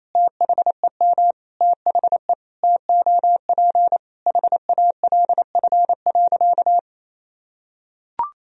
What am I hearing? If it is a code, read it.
Morse code, heard as a keyed tone, about 28 wpm, 700 Hz. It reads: THEM THE TOP HALF.
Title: Sentences from Top 1000 Words - Review Part 2 - 28wpm